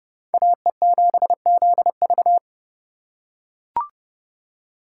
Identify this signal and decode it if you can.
Morse code, heard as a keyed tone, about 30 wpm, 700 Hz. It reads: AE7ZV